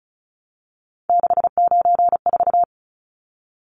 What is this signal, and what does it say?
Morse code, heard as a keyed tone, about 34 wpm, 700 Hz. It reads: 694
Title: Numbers - 3-Digit 35wpm